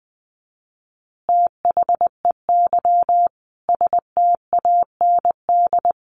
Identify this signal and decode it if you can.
Morse code, heard as a keyed tone, about 20 wpm, 700 Hz. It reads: THEY STAND
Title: Easy Phrases 20wpm